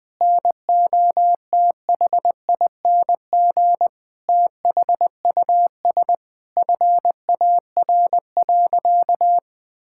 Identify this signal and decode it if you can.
Morse code, heard as a keyed tone, about 20 wpm, 700 Hz. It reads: NOTHING THUS FAR.
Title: Sentences from Top 1000 Words - Review Part 5 - 20wpm